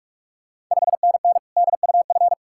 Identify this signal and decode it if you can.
Morse code, heard as a keyed tone, about 46 wpm, 700 Hz. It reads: 5NN DUF
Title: Ontario QSO Party Exchange 45wpm